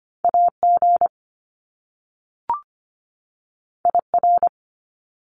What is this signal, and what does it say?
Morse code, heard as a keyed tone, about 25 wpm, 700 Hz. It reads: AG IR
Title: Top Two Letter Combinations in Words 25wpm